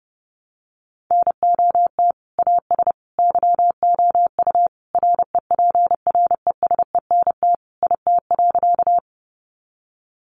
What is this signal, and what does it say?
Morse code, heard as a keyed tone, about 30 wpm, 700 Hz. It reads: NOT AS YOU REPRESENT IT.
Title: Sentences from Top 1200 Words 30wpm